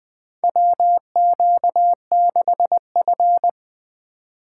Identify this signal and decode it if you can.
Morse code, heard as a keyed tone, about 20 wpm, 700 Hz. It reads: WQ6F
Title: USA Call Signs - 4 Characters - 4x Character Spacing - 20wpm